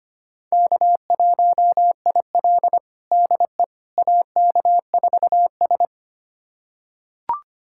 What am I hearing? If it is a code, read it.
Morse code, heard as a keyed tone, about 25 wpm, 700 Hz. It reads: K1IL DE AK4S
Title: USA Callsign DE Callsign - 4 Characters - 25wpm